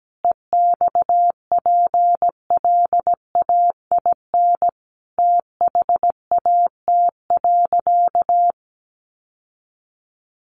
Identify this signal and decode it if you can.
Morse code, heard as a keyed tone, about 17 wpm, 700 Hz. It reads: EXPLAIN THAT.
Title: Sentences from Top 1100 Words 17wpm